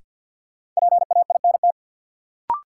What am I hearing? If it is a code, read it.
Morse code, heard as a keyed tone, about 53 wpm, 700 Hz. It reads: PAINT
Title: Sentences from Top 1000 Words - Review Part 6 - New Word Drill - 50wpm